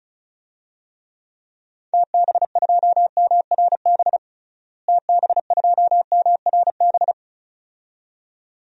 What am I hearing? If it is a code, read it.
Morse code, heard as a keyed tone, about 34 wpm, 700 Hz. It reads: TB2MRB TB2MRB
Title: DXCC-100 Call Signs - 6 Characters - Repeated 2x - 35wpm